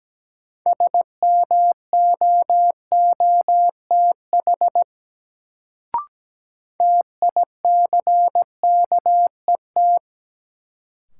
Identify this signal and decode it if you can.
Morse code, heard as a keyed tone, about 17 wpm, 700 Hz. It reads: SMOOTH TICKET
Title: Sentences from Top 1400 Words - New Word Drill Alt - 17wpm